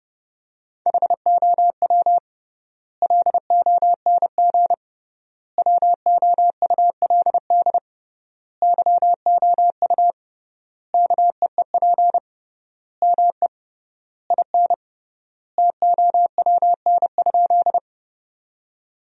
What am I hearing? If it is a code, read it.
Morse code, heard as a keyed tone, about 32 wpm, 700 Hz. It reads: HOW LONG WOULD YOU KEEP ME IN TOWN?
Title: 3x Word-Spacing - Sentences from Top 500 Words - 30wpm